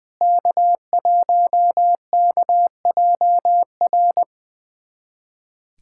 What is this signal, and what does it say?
Morse code, heard as a keyed tone, about 20 wpm, 700 Hz. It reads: K1KJR
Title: USA Callsign DE Callsign 20wpm